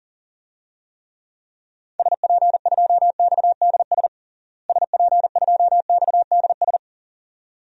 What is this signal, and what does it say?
Morse code, heard as a keyed tone, about 40 wpm, 700 Hz. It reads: SP2XDS SP2XDS
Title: DXCC-100 Call Signs - 6 Characters - Repeated 2x - 40wpm